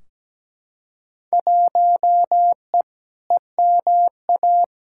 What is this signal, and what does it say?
Morse code, heard as a keyed tone, about 17 wpm, 700 Hz. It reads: 1E EMA
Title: Field Day Exchanges Encore - 17wpm